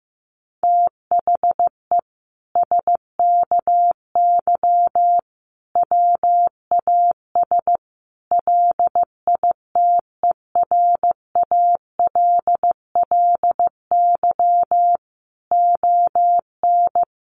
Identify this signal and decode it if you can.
Morse code, heard as a keyed tone, about 15 wpm, 700 Hz. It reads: THE SKY WAS LITERALLY ON
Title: Sentences from Top 1400 Words 15wpm